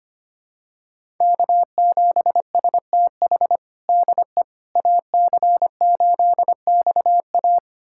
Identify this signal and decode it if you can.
Morse code, heard as a keyed tone, about 25 wpm, 700 Hz. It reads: K7STH DE AC8XA
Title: USA Callsign DE Callsign - 5 Characters - 25wpm